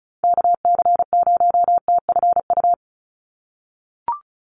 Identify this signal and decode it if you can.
Morse code, heard as a keyed tone, about 34 wpm, 700 Hz. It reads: KC0TFU
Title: USA Call Signs 35wpm